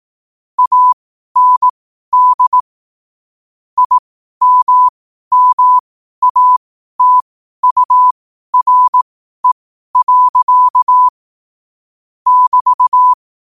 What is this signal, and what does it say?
Morse code, heard as a keyed tone, about 18 wpm, spaced as for 12 wpm, 1000 Hz. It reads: AND IMMATURE. <BT>
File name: qotdfiles / 12wpm